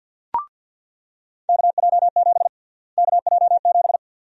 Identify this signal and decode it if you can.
Morse code, heard as a keyed tone, about 47 wpm, 700 Hz. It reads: KJ7 KJ7
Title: USA Call Sign Prefixes - Repeated 2x - 50wpm